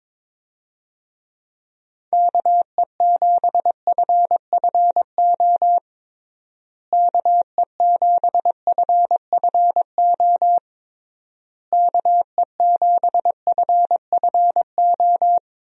Morse code, heard as a keyed tone, about 22 wpm, 700 Hz. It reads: KE7FFO KE7FFO KE7FFO
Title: USA Call Signs - 6 Characters - Repeated 3x - 22wpm